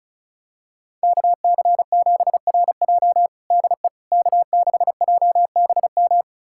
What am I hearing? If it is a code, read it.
Morse code, heard as a keyed tone, about 34 wpm, 700 Hz. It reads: KC7RJ DE K6JBM
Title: USA Callsign DE Callsign - 5 Characters - 35wpm